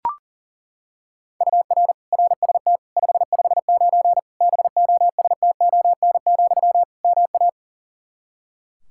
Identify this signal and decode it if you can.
Morse code, heard as a keyed tone, about 40 wpm, 700 Hz. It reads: UR RST 559 BOSTON, MA
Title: K1USN SST QSO Practice 40wpm